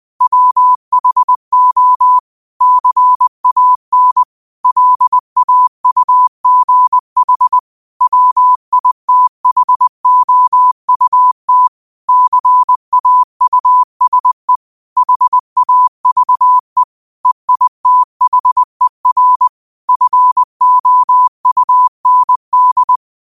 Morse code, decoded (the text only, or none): WHO CAN LAUGH WITHOUT CAUSE HAVE EITHER FOUND